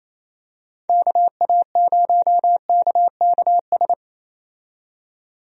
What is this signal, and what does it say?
Morse code, heard as a keyed tone, about 28 wpm, 700 Hz. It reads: KA0KKS
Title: USA Call Signs - 6 Characters - 28wpm